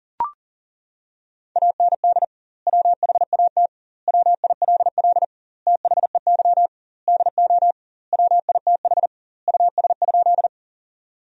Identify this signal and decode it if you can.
Morse code, heard as a keyed tone, about 40 wpm, 700 Hz. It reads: AND WHAT WILL THEY DO WITH US?
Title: 2x Word-Spacing - Sentences from Top 200 Words - 40wpm